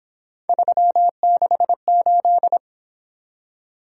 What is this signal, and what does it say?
Morse code, heard as a keyed tone, about 26 wpm, 700 Hz. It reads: 368
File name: Speed-Racing - Three Letter-Number Combinations - 22wpm